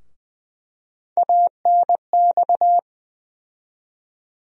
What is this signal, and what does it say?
Morse code, heard as a keyed tone, about 20 wpm, 700 Hz. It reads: ANX
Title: USA Call Sign Postfixes - 3 Letters 20wpm